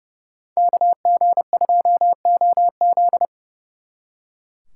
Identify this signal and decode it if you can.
Morse code, heard as a keyed tone, about 38 wpm, 700 Hz. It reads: KG2OZ